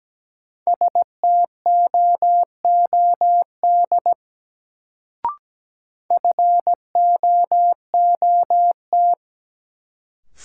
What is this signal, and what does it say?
Morse code, heard as a keyed tone, about 17 wpm, 700 Hz. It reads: STOOD FOOT
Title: Sentences from Top 500 Words - New Word Drill - 17wpm